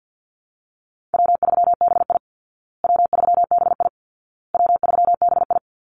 Morse code, heard as a keyed tone, about 50 wpm, 700 Hz. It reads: W3BI W3BI W3BI
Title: Easy Call Signs Repeated 3x - 50wpm